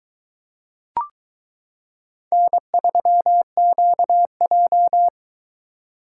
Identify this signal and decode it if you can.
Morse code, heard as a keyed tone, about 22 wpm, 700 Hz. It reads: N3QJ